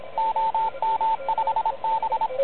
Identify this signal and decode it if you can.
Morse code, heard as a keyed tone, about 26 wpm, 890 Hz, with radio noise. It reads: OM5B
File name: truSDX / MWC29.8